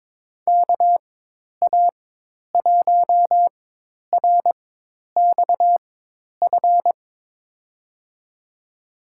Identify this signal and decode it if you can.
Morse code, heard as a keyed tone, about 22 wpm, spaced as for 10 wpm, 700 Hz. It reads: KA1RXF
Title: USA Call Signs - 4x Character Spacing - 22wpm